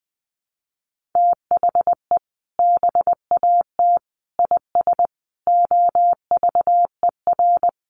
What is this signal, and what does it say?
Morse code, heard as a keyed tone, about 20 wpm, 700 Hz. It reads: THE BAT IS OVER
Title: Sentences from Top 1500 Words 20wpm